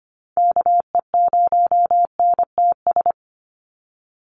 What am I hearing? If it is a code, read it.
Morse code, heard as a keyed tone, about 25 wpm, 700 Hz. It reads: KE0NTS
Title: USA Call Signs 25wpm